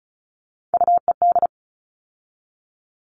Morse code, heard as a keyed tone, about 34 wpm, 700 Hz. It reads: UED
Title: All Cut Numbers - 3-Digit 35wpm